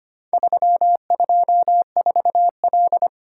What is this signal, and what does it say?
Morse code, heard as a keyed tone, about 25 wpm, 700 Hz. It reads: 324L
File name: practice / 25wpm / Four Letter-Number Combinations - 25wpm